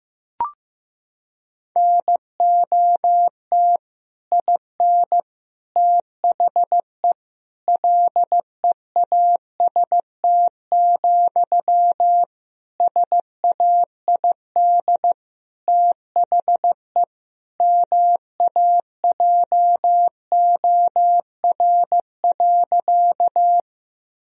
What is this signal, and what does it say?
Morse code, heard as a keyed tone, about 15 wpm, 700 Hz. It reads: NOT IN THE LEAST, SAID THE MAJOR.
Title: Sentences from Top 1000 Words - Review Part 4 - 15wpm